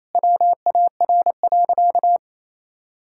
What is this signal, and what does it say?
Morse code, heard as a keyed tone, about 28 wpm, 700 Hz. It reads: WAR.
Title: Sentences from Top 1600 Words 28wpm